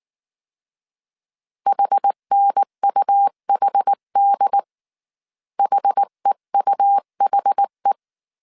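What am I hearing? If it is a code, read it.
Morse code, heard as a keyed tone, about 19 wpm, 800 Hz. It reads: HNUHD HEUHE
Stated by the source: Morse code dictation at 18 wpm from the IW3SGT QRS Morse code training course, five-character groups of 300 characters.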